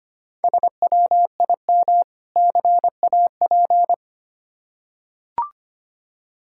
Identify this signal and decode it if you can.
Morse code, heard as a keyed tone, about 25 wpm, 700 Hz. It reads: SWIM CAP
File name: practice / 25wpm / Difficult Words with Hints - 25wpm